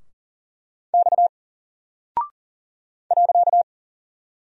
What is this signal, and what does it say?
Morse code, heard as a keyed tone, about 40 wpm, 700 Hz. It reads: X .
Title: Single Character (Letter:Number:Punctuation:Prosign) Alt 40wpm